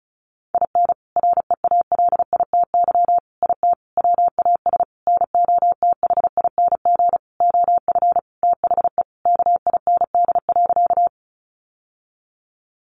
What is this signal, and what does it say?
Morse code, heard as a keyed tone, about 34 wpm, 700 Hz. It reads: IN REALITY IT WAS NOTHING OF THE KIND.
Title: Sentences from Top 1200 Words 35wpm